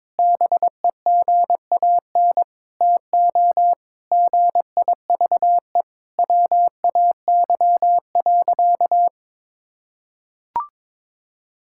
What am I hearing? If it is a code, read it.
Morse code, heard as a keyed tone, about 22 wpm, 700 Hz. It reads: BEGAN TO GIVE WAY.
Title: Sentences from Top 1600 Words 22wpm